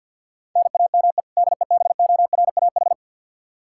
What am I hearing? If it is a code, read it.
Morse code, heard as a keyed tone, about 52 wpm, 700 Hz. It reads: NAME DEBORAH
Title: Practice QSO - Exchange 1 - Beginning - 50wpm